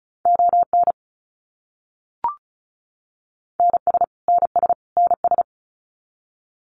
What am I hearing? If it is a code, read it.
Morse code, heard as a keyed tone, about 34 wpm, 700 Hz. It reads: ON NS NS NS
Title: Canadian Provinces and Territories - Repeated 3x - 35wpm